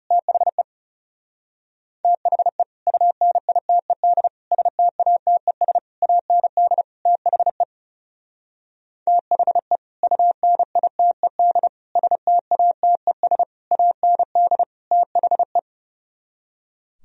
Morse code, heard as a keyed tone, about 36 wpm, 700 Hz. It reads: THE THE UNITED STATES AND THE THE UNITED STATES AND THE
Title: Speed-Racing - Sets of 5 Words 30wpm